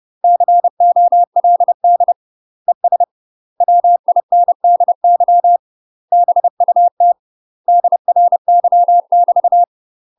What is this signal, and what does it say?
Morse code, heard as a keyed tone, about 30 wpm, 700 Hz. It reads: COLD ES WINDY BUT DRY <BT>
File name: UK MASTER QSOs TRAINING / 30WPM / UK MASTER QSOs 0040